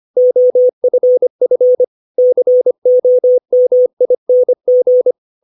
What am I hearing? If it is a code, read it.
Morse code, heard as a keyed tone, about 25 wpm, 500 Hz. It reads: OFF COMING
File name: 632 words / 632 Words 25WPM